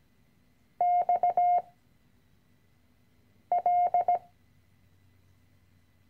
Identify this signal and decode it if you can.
Morse code, heard as a keyed tone, about 17 wpm, 710 Hz. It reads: X L
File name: CW-Online-Kurs / Anfänger / Lektion 010